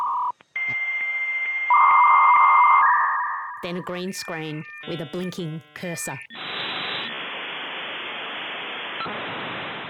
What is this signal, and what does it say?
Dial-up modem handshake, with speech.